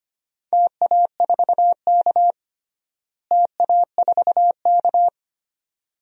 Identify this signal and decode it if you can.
Morse code, heard as a keyed tone, about 25 wpm, 700 Hz. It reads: TA4K TA4K
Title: DXCC-100 Call Signs - 4 Characters - Repeated 2x - 25wpm